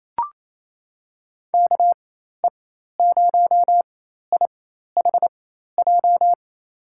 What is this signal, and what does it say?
Morse code, heard as a keyed tone, about 28 wpm, spaced as for 13 wpm, 700 Hz. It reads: KE0IHJ
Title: USA Call Signs - 6 Characters - 4x Character Spacing - 28wpm